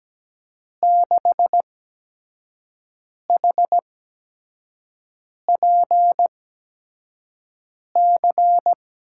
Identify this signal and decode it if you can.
Morse code, heard as a keyed tone, about 17 wpm, spaced as for 5 wpm, 700 Hz. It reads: 6HPC
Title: USA Call Signs - 8x Character Spacing - 17wpm